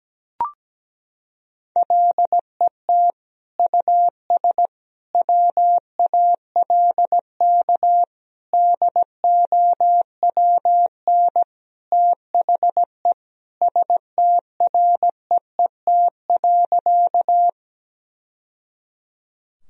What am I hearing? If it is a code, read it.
Morse code, heard as a keyed tone, about 17 wpm, 700 Hz. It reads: LET US WALK DOWN THE STREET.
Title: Sentences from Top 1000 Words - Review Part 3 - 17wpm